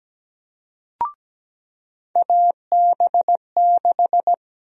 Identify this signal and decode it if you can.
Morse code, heard as a keyed tone, about 17 wpm, 700 Hz. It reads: AB6
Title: USA Call Sign Prefixes - 17wpm